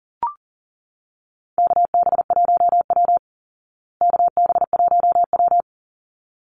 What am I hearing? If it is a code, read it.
Morse code, heard as a keyed tone, about 40 wpm, 700 Hz. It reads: KB1W KB1W
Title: Call Signs Repeated 3x - 40wpm